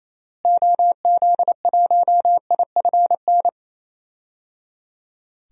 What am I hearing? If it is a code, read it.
Morse code, heard as a keyed tone, about 27 wpm, 700 Hz. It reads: OZ1IFN
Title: DXCC-100 Call Signs - 6 Characters - Repeated 3x - 28wpm